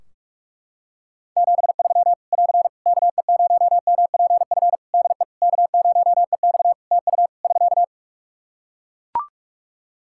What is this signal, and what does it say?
Morse code, heard as a keyed tone, about 41 wpm, 700 Hz. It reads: 73 <AR> KE0MPF DE K0EX TU <SK>
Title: Practice QSO - Exchange 4 - Ending - 45wpm